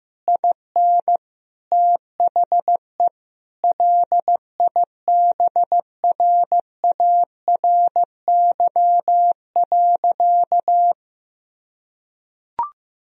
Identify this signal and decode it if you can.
Morse code, heard as a keyed tone, about 15 wpm, 700 Hz. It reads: IN THE LIBRARY.
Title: Sentences from Top 1000 Words - Review Part 4 - 15wpm